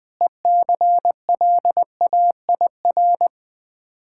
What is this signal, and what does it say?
Morse code, heard as a keyed tone, about 20 wpm, 700 Hz. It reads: ECLAIR
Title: Difficult Words with Hints - 20wpm